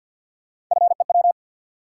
Morse code, heard as a keyed tone, about 49 wpm, 700 Hz. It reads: FEW
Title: Sentences from Top 300 Words - New Word Drill - 50wpm